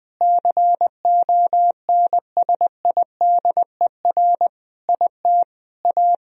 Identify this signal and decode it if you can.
Morse code, heard as a keyed tone, about 20 wpm, 700 Hz. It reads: CONSIDER IT A
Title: Sentences from Top 1000 Words - Review Part 4 - 20wpm